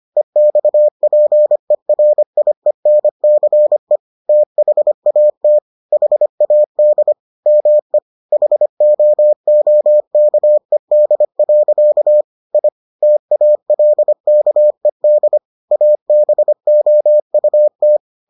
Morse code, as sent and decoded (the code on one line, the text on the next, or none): . -..- .--. .-. .. -. -.-. / - .... .- - / .... .- -.. / -- . / .... --- --- -.- . -.. .-.-.- / .. / - .- .-.. -.- . -.. / .- -... --- ..- -
EXPERIENCE THAT HAD ME HOOKED. I TALKED ABOUT